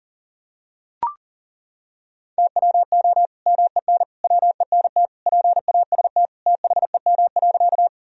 Morse code, heard as a keyed tone, about 40 wpm, 700 Hz. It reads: TWO MEN WENT PAST THEM.
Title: Sentences from Top 1000 Words - Review Part 3 - 40wpm